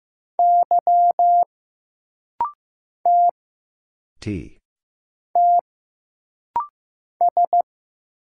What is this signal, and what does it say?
Morse code, heard as a keyed tone, about 15 wpm, 700 Hz. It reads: Y T T S
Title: Single Letter-Number Alt - Rapid-Fire 15wpm